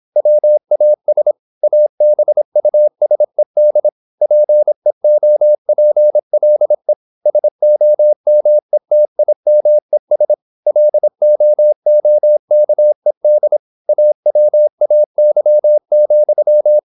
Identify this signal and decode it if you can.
Morse code, heard as a keyed tone, about 26 wpm, 600 Hz. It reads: WAS ABUSED PEOPLE SOMETIMES LOOKED AWAY,
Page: Never 26wpm | CW med Gnister